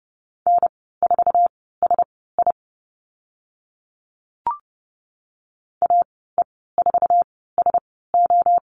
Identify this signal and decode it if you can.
Morse code, heard as a keyed tone, about 30 wpm, spaced as for 17 wpm, 700 Hz. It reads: N4SI AE4SO
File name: Lesson 15g - Number 5 - Call Signs 30wpm